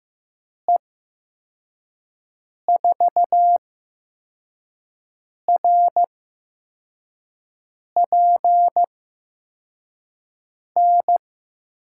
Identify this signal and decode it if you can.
Morse code, heard as a keyed tone, about 15 wpm, spaced as for 4 wpm, 700 Hz. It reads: E4RPN
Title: USA Call Signs - 6 Characters - 8x Character Spacing - 15wpm